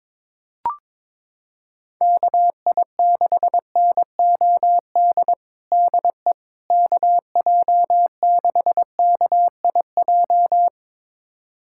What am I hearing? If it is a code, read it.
Morse code, heard as a keyed tone, about 22 wpm, 700 Hz. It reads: KI6NOD DE KJ6KIJ
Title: USA Callsign DE Callsign - 6 Characters - 22wpm